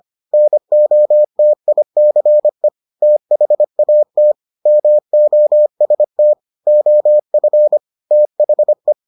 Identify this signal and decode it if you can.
Morse code, heard as a keyed tone, about 25 wpm, 600 Hz. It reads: NOTICE THAT MOST OF THE